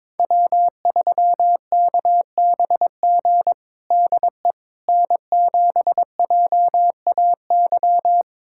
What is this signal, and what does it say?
Morse code, heard as a keyed tone, about 22 wpm, 700 Hz. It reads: W3KBG DE N7JAY
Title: USA Callsign DE Callsign - 5 Characters - 22wpm